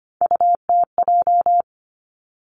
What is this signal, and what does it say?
Morse code, heard as a keyed tone, about 25 wpm, 700 Hz. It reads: UTJ
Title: Three Challenging Letter-Number Combinations 25wpm